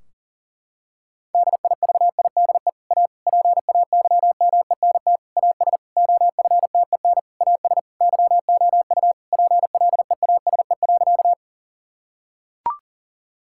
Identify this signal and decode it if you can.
Morse code, heard as a keyed tone, about 40 wpm, 700 Hz. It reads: DIVIDE A PAYMENT AS OFTEN AS YOU PLEASE.
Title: Sentences from Top 1600 Words 40wpm